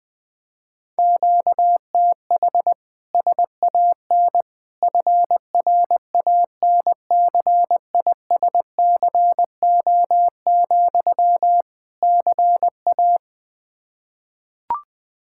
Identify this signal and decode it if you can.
Morse code, heard as a keyed tone, about 20 wpm, 700 Hz. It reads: QTH SAN FRANCISCO, CA
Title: Practice QSO - Exchange 1 - Beginning - 20wpm